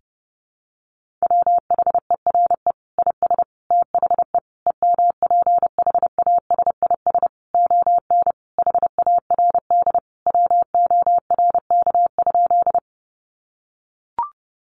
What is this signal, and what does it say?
Morse code, heard as a keyed tone, about 30 wpm, 700 Hz. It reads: WHERE IS THE EMPHASIS ON HARD WORK?
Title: Sentences from Top 1400 Words 30wpm